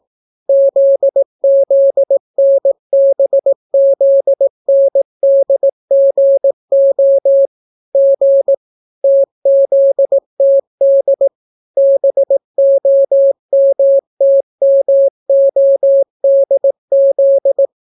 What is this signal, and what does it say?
Morse code, heard as a keyed tone, about 18 wpm, 550 Hz. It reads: ZZNBZNDGO G TZTD BOMTMODZ